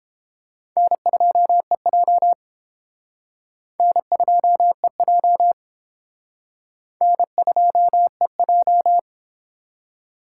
Morse code, heard as a keyed tone, about 29 wpm, 700 Hz. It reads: N2EJ N2EJ N2EJ